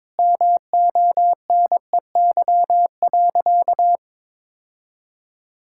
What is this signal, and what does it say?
Morse code, heard as a keyed tone, about 21 wpm, 700 Hz. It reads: MONEY.
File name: practice / 22wpm / Sentences with 4-Word Phrases - 22wpm